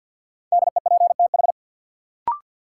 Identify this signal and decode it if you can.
Morse code, heard as a keyed tone, about 53 wpm, 700 Hz. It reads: DEPTH
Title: Sentences from Top 1200 Words - New Word Drill Alt - 50wpm